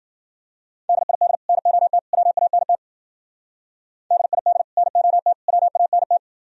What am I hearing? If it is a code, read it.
Morse code, heard as a keyed tone, about 48 wpm, 700 Hz. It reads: DID NOT WANT DID NOT WANT
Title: Speed-Racing - Sets of 3 Words 40wpm